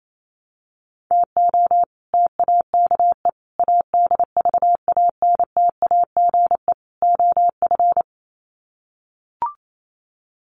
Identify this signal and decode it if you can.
Morse code, heard as a keyed tone, about 28 wpm, 700 Hz. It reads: TO TAKE ADVANTAGE OF
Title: Sets of 4 Words 28wpm